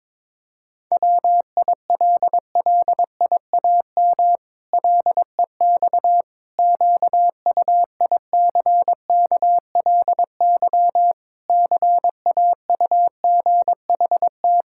Morse code, heard as a keyed tone, about 21 wpm, 700 Hz. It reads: WILLIAM LEX QUICKLY CAUGHT
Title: Pangram Challenge v2 - 22wpm